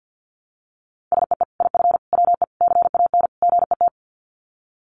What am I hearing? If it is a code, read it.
Morse code, heard as a keyed tone, about 50 wpm, 700 Hz. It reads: SEE IF WE CAN GET
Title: Sets of 5 Words Encore2 50wpm